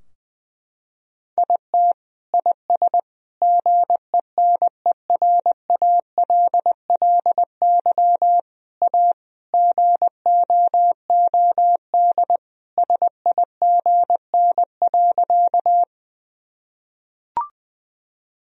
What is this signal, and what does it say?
Morse code, heard as a keyed tone, about 21 wpm, 700 Hz. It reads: IT IS GENERALLY A GOOD SIGN.
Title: Sentences from Top 1000 Words - Review Part 4 - 20wpm